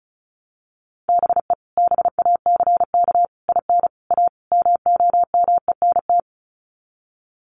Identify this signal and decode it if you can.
Morse code, heard as a keyed tone, about 34 wpm, 700 Hz. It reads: BE BACK IN A MOMENT
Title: Sets of 5 Words 35wpm